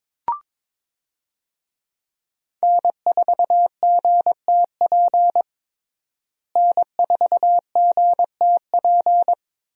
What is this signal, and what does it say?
Morse code, heard as a keyed tone, about 22 wpm, 700 Hz. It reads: N4GTP N4GTP
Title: USA Call Signs - 5 Characters - Repeated 2x - 22wpm